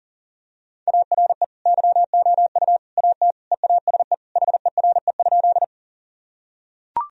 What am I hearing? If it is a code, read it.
Morse code, heard as a keyed tone, about 40 wpm, 700 Hz. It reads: ARE YOU AT EASE HERE?
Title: Sentences from Top 1000 Words - Review Part 1 - 40wpm